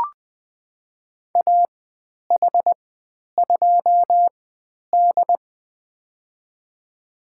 Morse code, heard as a keyed tone, about 20 wpm, spaced as for 10 wpm, 700 Hz. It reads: AH2D
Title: 109 Lesson 18g - Letter m - Call Signs 20wpm